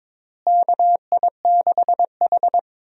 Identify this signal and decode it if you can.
Morse code, heard as a keyed tone, about 22 wpm, 700 Hz. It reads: KI6H
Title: USA Call Signs - 4 Characters - Repeated 3x - 22wpm